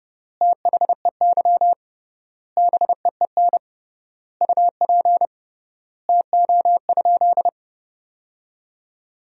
Morse code, heard as a keyed tone, about 30 wpm, 700 Hz. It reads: THEY BEEN UP TO?
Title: 3x Word-Spacing - Sentences from Top 100 Words - 30wpm